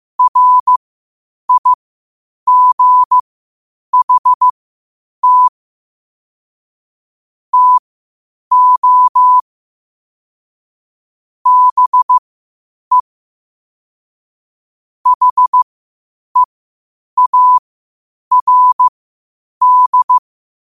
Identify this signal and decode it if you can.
Morse code, heard as a keyed tone, about 15 wpm, spaced as for 8 wpm, 1000 Hz. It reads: RIGHT TO BE HEARD
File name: qotdfiles / 8wpm